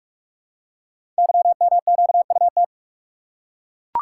Morse code, heard as a keyed tone, about 44 wpm, 690 Hz. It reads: YMQUT
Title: Code Groups - 5 Letters - 45wpm